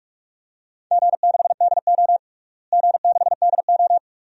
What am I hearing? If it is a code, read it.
Morse code, heard as a keyed tone, about 41 wpm, 700 Hz. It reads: G6DO G6DO
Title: DXCC-100 Call Signs - 4 Characters - Repeated 2x - 45wpm